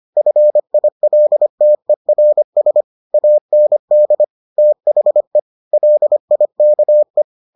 Morse code, heard as a keyed tone, about 25 wpm, 600 Hz. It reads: FILTERS AND THE LIKE